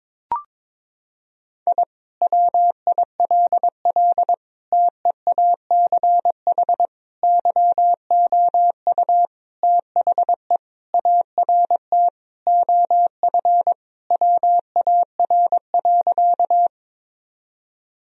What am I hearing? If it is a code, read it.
Morse code, heard as a keyed tone, about 22 wpm, 700 Hz. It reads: I WILL TEACH YOU THE ART OF WAR.
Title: Sentences from Top 1000 Words - Review Part 3 - 22wpm